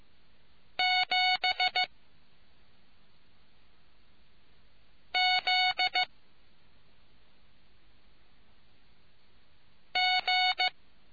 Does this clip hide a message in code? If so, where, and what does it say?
Yes: Morse code from 0.8 to 10.7 s, heard as a keyed tone, about 15 wpm, 740 Hz. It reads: 7 Z G